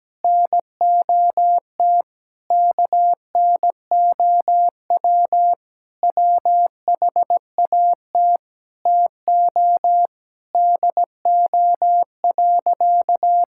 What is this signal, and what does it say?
Morse code, heard as a keyed tone, about 17 wpm, 700 Hz. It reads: NOT KNOW WHAT TO DO.
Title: Sentences from Top 1000 Words - Review Part 5 - 17wpm